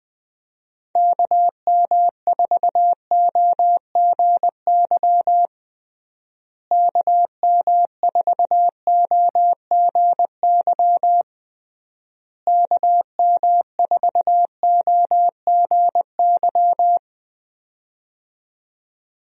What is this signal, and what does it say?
Morse code, heard as a keyed tone, about 20 wpm, 700 Hz. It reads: KM4OGY KM4OGY KM4OGY
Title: USA Call Signs - 6 Characters - Repeated 3x - 20wpm